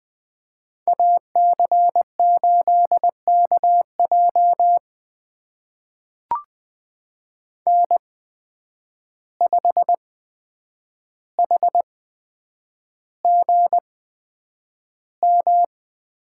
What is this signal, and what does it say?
Morse code, heard as a keyed tone, about 20 wpm, spaced as for 9 wpm, 700 Hz. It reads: AC8KJ N5HGM
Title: USA Call Signs - 5 Characters - 8x Character Spacing - 20wpm